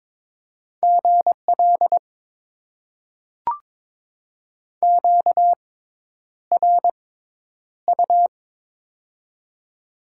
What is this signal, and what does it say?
Morse code, heard as a keyed tone, about 22 wpm, spaced as for 10 wpm, 700 Hz. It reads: GL QRU
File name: Top 90 QSO Elements - 6x Character Spacing - 22wpm